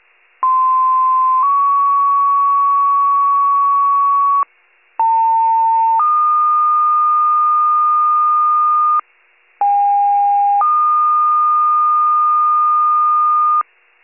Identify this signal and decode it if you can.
Two-tone page, heard as a one-second tone, then a three-second tone, about 1040 then 1120 Hz, about 900 then 1220 Hz, about 810 then 1220 Hz. It is heard from 0.4 to 13.6 s.